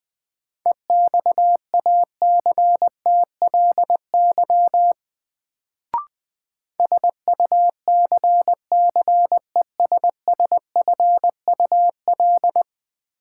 Morse code, heard as a keyed tone, about 20 wpm, 700 Hz. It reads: EXACTLY SUCCESSFUL
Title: Sentences from Top 900 Words - New Word Drill - 20wpm